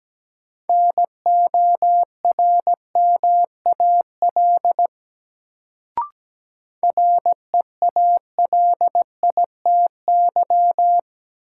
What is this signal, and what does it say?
Morse code, heard as a keyed tone, about 17 wpm, 700 Hz. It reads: NORMAL REALITY
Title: Sentences from Top 1200 Words - New Word Drill - 17wpm